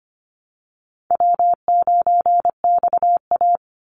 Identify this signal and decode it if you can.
Morse code, heard as a keyed tone, about 25 wpm, 700 Hz. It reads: W9XA
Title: Call Signs Repeated 2x - 25wpm